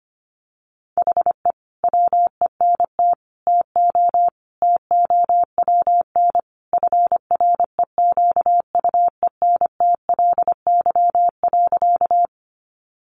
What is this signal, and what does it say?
Morse code, heard as a keyed tone, about 25 wpm, 700 Hz. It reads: HE WENT TO TOWN FREQUENTLY.
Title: Sentences from Top 1300 Words 25wpm